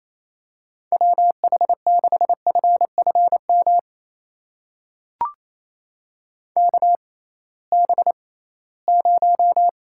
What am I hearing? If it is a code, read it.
Morse code, heard as a keyed tone, about 28 wpm, spaced as for 18 wpm, 700 Hz. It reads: WH6FFM KB0
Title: USA Call Signs - 6 Characters - 6x Character Spacing - 28wpm